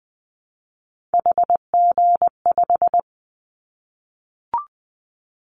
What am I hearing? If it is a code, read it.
Morse code, heard as a keyed tone, about 20 wpm, 700 Hz. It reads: HG5
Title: Callsign Prefixes for Top 25 DXCC - Repeated 3x - 20wpm